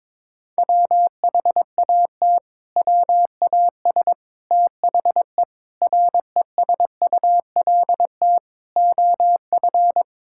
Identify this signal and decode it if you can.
Morse code, heard as a keyed tone, about 22 wpm, 700 Hz. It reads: WHAT WAS THE RESULT OF
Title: Sentences from Top 1000 Words - Review Part 5 - 22wpm